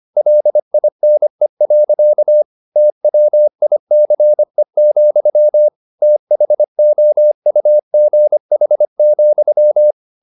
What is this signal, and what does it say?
Morse code, heard as a keyed tone, about 25 wpm, 600 Hz. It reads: LINE. TWICE, THOUGH,